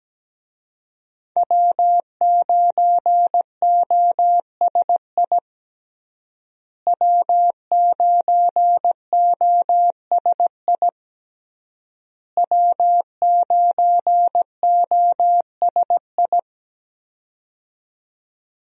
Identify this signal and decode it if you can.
Morse code, heard as a keyed tone, about 17 wpm, 700 Hz. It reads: W9OSI W9OSI W9OSI
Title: USA Call Signs - 5 Characters - Repeated 3x - 17wpm